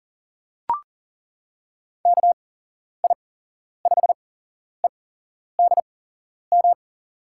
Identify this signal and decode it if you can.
Morse code, heard as a keyed tone, about 38 wpm, spaced as for 11 wpm, 700 Hz. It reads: KI5EDM
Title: USA Call Signs - 6 Characters - 8x Character Spacing - 40wpm